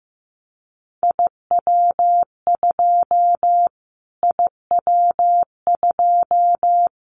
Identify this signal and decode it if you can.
Morse code, heard as a keyed tone, about 15 wpm, 700 Hz. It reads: IW2 IW2
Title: Callsign Prefixes for Top 25 DXCC - Repeated 3x - 15wpm